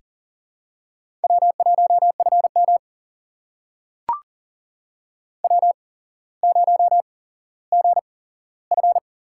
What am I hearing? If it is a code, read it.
Morse code, heard as a keyed tone, about 40 wpm, spaced as for 17 wpm, 700 Hz. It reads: W1FM W0GF
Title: USA Call Signs - 4 Characters - 8x Character Spacing - 40wpm